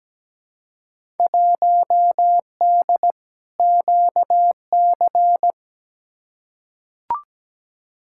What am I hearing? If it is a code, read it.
Morse code, heard as a keyed tone, about 17 wpm, 700 Hz. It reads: 1D QC
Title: Field Day Exchanges Encore - 17wpm